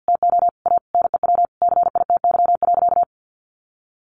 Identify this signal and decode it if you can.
Morse code, heard as a keyed tone, about 48 wpm, 700 Hz. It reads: TO A NEW CITY.
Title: Easy Phrases 50wpm